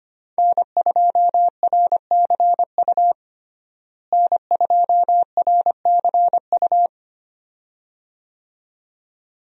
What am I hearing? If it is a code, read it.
Morse code, heard as a keyed tone, about 25 wpm, 700 Hz. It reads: N2RCU N2RCU
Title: USA Call Signs - 5 Characters - Repeated 2x - 25wpm